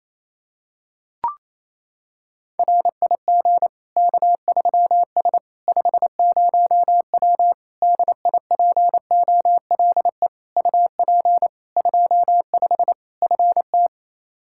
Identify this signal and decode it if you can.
Morse code, heard as a keyed tone, about 28 wpm, 700 Hz. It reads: RIG K3S 50W DIPOLE UP 25 FT
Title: Practice QSO - Exchange 2 - Info - 28wpm